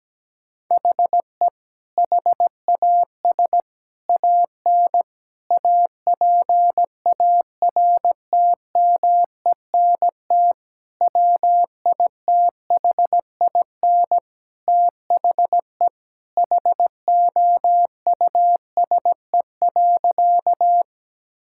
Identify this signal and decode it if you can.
Morse code, heard as a keyed tone, about 17 wpm, 700 Hz. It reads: HE HAS AN APARTMENT WITHIN THE HOUSE.
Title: Sentences from Top 1000 Words - Review Part 6 - 17wpm